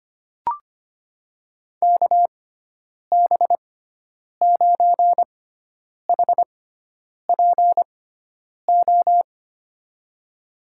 Morse code, heard as a keyed tone, about 25 wpm, spaced as for 9 wpm, 700 Hz. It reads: KB9HPO
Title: USA Call Signs - 6x Character Spacing - 25wpm